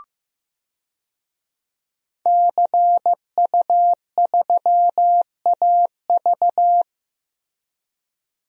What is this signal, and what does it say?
Morse code, heard as a keyed tone, about 15 wpm, 700 Hz. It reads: CU3AV
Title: DXCC-100 Call Signs - 5 Characters - Repeated 2x - 15wpm